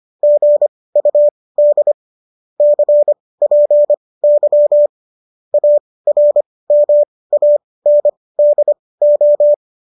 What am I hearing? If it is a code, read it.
Morse code, heard as a keyed tone, about 25 wpm, spaced as for 18 wpm, 590 Hz. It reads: GUD CPY ARMANDO
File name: qso215 18